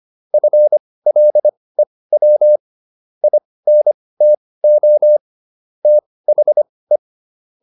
Morse code, heard as a keyed tone, about 25 wpm, spaced as for 18 wpm, 600 Hz. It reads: FLEW INTO THE